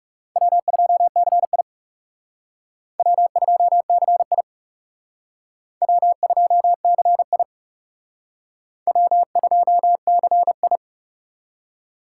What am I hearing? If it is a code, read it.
Morse code, heard as a keyed tone, about 37 wpm, 700 Hz. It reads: W2CI W2CI W2CI W2CI
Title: Speed Racing - USA 1x2 Call Signs 30wpm